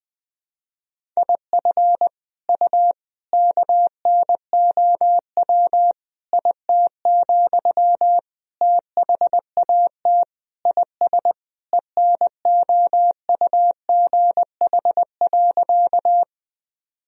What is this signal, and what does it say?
Morse code, heard as a keyed tone, about 20 wpm, 700 Hz. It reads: IF U KNOW IT, THAT IS ENOUGH.
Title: Sentences from Top 400 Words and Abbreviations 20wpm